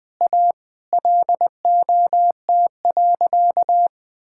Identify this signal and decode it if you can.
Morse code, heard as a keyed tone, about 20 wpm, 700 Hz. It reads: A LOT.
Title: Sentences with 3-Word Phrases - 20wpm